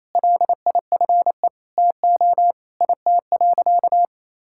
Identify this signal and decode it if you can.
Morse code, heard as a keyed tone, about 27 wpm, 700 Hz. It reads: LIFE TO IT.
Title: Sentences from Top 1000 Words - Review Part 5 - 28wpm